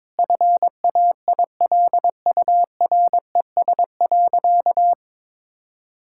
Morse code, heard as a keyed tone, about 22 wpm, 700 Hz. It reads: FAILURES.
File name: Sentences with 7-Word Phrases - 22wpm